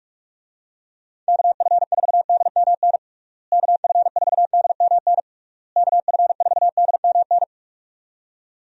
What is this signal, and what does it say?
Morse code, heard as a keyed tone, about 41 wpm, 700 Hz. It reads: KF4DMN KF4DMN KF4DMN
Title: USA Call Signs - 6 Characters - Repeated 3x - 45wpm